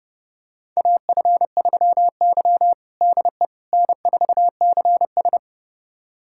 Morse code, heard as a keyed tone, about 30 wpm, 700 Hz. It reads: AF3Y DE N4CS
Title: USA Callsign DE Callsign - 4 Characters - 30wpm